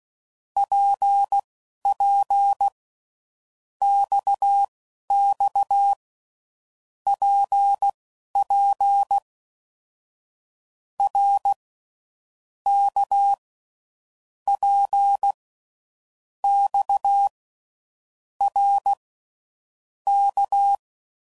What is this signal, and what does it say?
Morse code, heard as a keyed tone, about 16 wpm, spaced as for 11 wpm, 800 Hz. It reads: PP XX PP R K P X R K
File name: Groups / Basic Letter Groups / RKPX